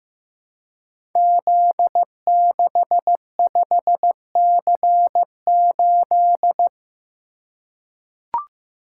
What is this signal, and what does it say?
Morse code, heard as a keyed tone, about 15 wpm, 700 Hz. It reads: Z65C8
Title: Code Groups - 5 Letter:Numbers - 15wpm